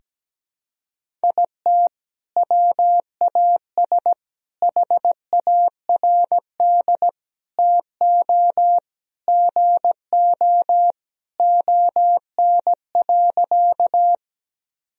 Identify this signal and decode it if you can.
Morse code, heard as a keyed tone, about 17 wpm, 700 Hz. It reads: IT WAS HARD TO GO ON.
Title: Sentences from Top 1000 Words - Review Part 1 - 17wpm